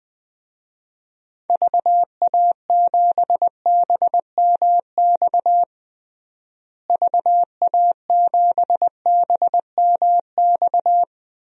Morse code, heard as a keyed tone, about 20 wpm, 700 Hz. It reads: VA7BMX VA7BMX
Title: DXCC-100 Call Signs - 6 Characters - Repeated 3x - 20wpm